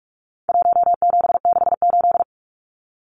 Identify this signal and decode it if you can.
Morse code, heard as a keyed tone, about 44 wpm, 700 Hz. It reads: 1768
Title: Numbers - Contest Style - 4-Digit 45wpm